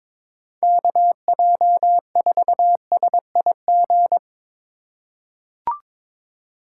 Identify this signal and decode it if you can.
Morse code, heard as a keyed tone, about 21 wpm, 700 Hz. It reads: KJ4SIG
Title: USA Call Signs - 6 Characters - 3x Character Spacing - 22wpm